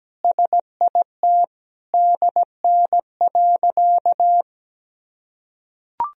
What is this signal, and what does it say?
Morse code, heard as a keyed tone, about 17 wpm, 700 Hz. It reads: SIT DN.
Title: Sentences from Top 500 Words and Abbreviations 17wpm